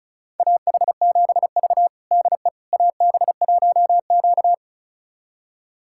Morse code, heard as a keyed tone, about 36 wpm, 700 Hz. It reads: AH7V DE AB1Q
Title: USA Callsign DE Callsign - 4 Characters - 35wpm